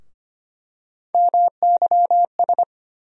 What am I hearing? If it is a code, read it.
Morse code, heard as a keyed tone, about 25 wpm, 700 Hz. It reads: MYS
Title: USA Call Sign Postfixes - 3 Letters 25wpm